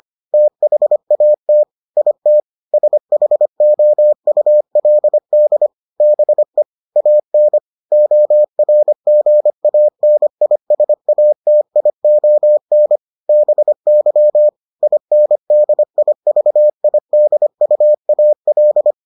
Morse code, heard as a keyed tone, about 25 wpm, 600 Hz. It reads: THAT IT SHOULD BE AN ORGANISATION BY INDIVIDUAL